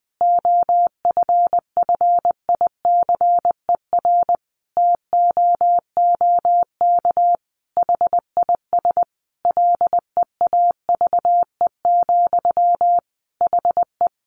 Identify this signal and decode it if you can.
Morse code, heard as a keyed tone, about 20 wpm, 700 Hz. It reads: OFFICER TOOK HIS LEAVE, HE
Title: Sentences from Top 1200 Words 20wpm